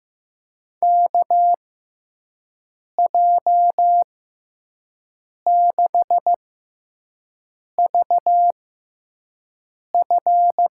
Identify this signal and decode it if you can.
Morse code, heard as a keyed tone, about 15 wpm, spaced as for 5 wpm, 700 Hz. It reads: KJ6VF
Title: USA Call Signs - 6 Characters - 6x Character Spacing - 15wpm